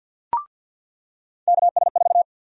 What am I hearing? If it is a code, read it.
Morse code, heard as a keyed tone, about 47 wpm, 700 Hz. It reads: KS4
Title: USA Call Sign Prefixes - 50wpm